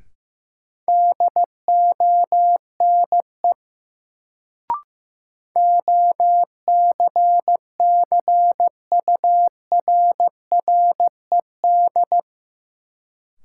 Morse code, heard as a keyed tone, about 15 wpm, 700 Hz. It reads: DONE OCCURRED
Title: Sets of 5 Words Encore - New Word Drill - 15wpm